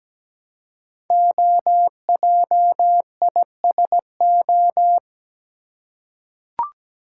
Morse code, heard as a keyed tone, about 17 wpm, 700 Hz. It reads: OJISO
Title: Code Groups - 5 Letters - 17wpm